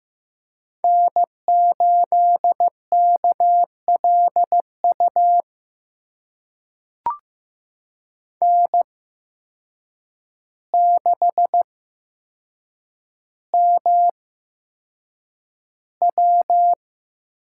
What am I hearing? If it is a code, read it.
Morse code, heard as a keyed tone, about 15 wpm, spaced as for 7 wpm, 700 Hz. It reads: N8KLU N6MW
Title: USA Call Signs - 5 Characters - 8x Character Spacing - 15wpm